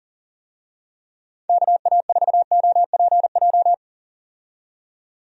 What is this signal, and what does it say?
Morse code, heard as a keyed tone, about 40 wpm, 700 Hz. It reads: KA4OPJ
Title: USA Call Signs - 6 Characters - 40wpm